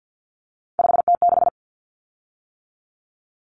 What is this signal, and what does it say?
Morse code, heard as a keyed tone, about 50 wpm, 700 Hz. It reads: 5T6
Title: Common Cut Numbers - 3-Digit 50wpm